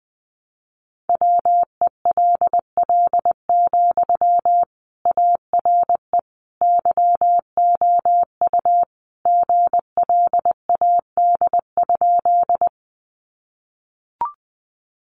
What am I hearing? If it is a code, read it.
Morse code, heard as a keyed tone, about 20 wpm, 700 Hz. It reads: WELL, ARE YOU GLAD?
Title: Sentences from Top 800 Words 20wpm